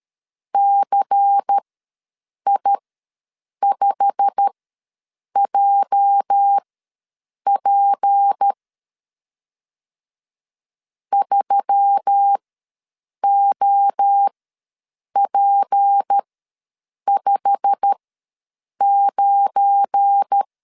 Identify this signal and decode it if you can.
Morse code, heard as a keyed tone, about 13 wpm, spaced as for 7 wpm, 800 Hz. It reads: CI5JP 3OP59
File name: LEZIONI / 8 YJ45 / 8 YJ45 ws 12wpm 3146